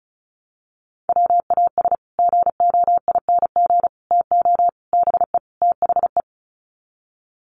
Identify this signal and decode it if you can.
Morse code, heard as a keyed tone, about 34 wpm, 700 Hz. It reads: WAS GOING TO BE THE
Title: Sets of 5 Words Encore2 35wpm